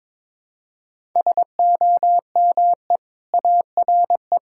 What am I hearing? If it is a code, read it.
Morse code, heard as a keyed tone, about 22 wpm, 700 Hz. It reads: SOME ARE
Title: Sentences from Top 1000 Words - Review Part 6 - 22wpm